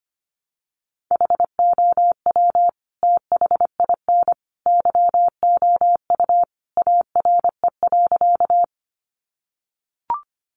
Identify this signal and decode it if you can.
Morse code, heard as a keyed tone, about 25 wpm, 700 Hz. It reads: HOW THIN YOU ARE.
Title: Sentences from Top 1200 Words 25wpm